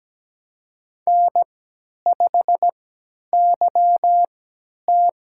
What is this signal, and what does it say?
Morse code, heard as a keyed tone, about 17 wpm, spaced as for 10 wpm, 700 Hz. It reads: N5YT
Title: USA Call Signs - 3x Character Spacing - 17wpm